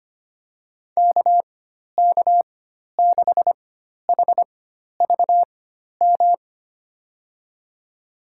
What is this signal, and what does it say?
Morse code, heard as a keyed tone, about 25 wpm, spaced as for 12 wpm, 700 Hz. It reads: KK6HVM
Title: USA Call Signs - 6 Characters - 4x Character Spacing - 25wpm